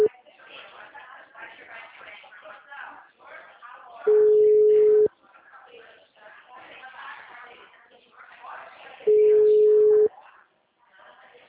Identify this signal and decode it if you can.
Ringback tone (Europe, 425 Hz), with speech.